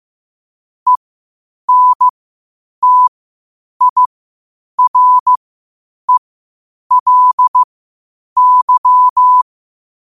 Morse code, heard as a keyed tone, about 15 wpm, spaced as for 8 wpm, 1000 Hz. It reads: ENTIRELY
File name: qotdfiles / 8wpm